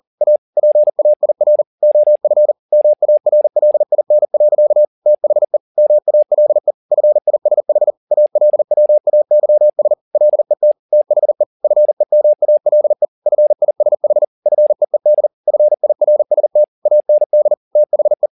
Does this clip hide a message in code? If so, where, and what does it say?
Yes: Morse code from 0.2 to 18.3 s, heard as a keyed tone, about 40 wpm, 600 Hz. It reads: A PAIR OF MARLIN. THE MALE FISH ALWAYS LET THE FEMALE FISH FEED FIRST AND THE